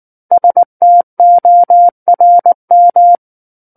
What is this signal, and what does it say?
Morse code, heard as a keyed tone, about 19 wpm, 700 Hz. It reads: STORM
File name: Book One / 10-Chapter 10 - 19 WPM